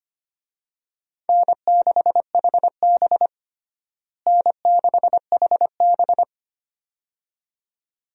Morse code, heard as a keyed tone, about 25 wpm, 700 Hz. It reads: N6HB N6HB
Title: USA Call Signs - 4 Characters - Repeated 2x - 25wpm